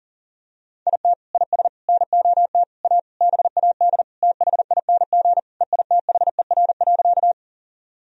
Morse code, heard as a keyed tone, about 40 wpm, 700 Hz. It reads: IT IS NOT A BAD THING EITHER.
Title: Sentences from Top 1000 Words - Review Part 4 - 40wpm